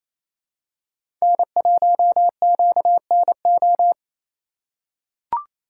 Morse code, heard as a keyed tone, about 28 wpm, 700 Hz. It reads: N1QNO
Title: USA Call Signs - 5 Characters - 4x Character Spacing - 28wpm